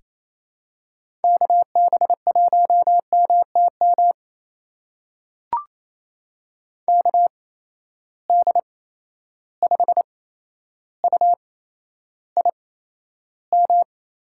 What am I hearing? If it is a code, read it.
Morse code, heard as a keyed tone, about 28 wpm, spaced as for 12 wpm, 700 Hz. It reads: KB1MTM KD5UIM
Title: USA Call Signs - 6 Characters - 8x Character Spacing - 28wpm